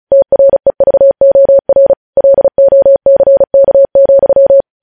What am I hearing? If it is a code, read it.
Morse code, heard as a keyed tone, about 36 wpm, 550 Hz. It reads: TREVOR LOCK,
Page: news headlines in morse code at 35 wpm